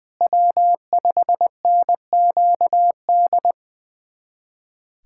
Morse code, heard as a keyed tone, about 20 wpm, 700 Hz. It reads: W5NQD
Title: USA Callsign DE Callsign - 5 Characters - 20wpm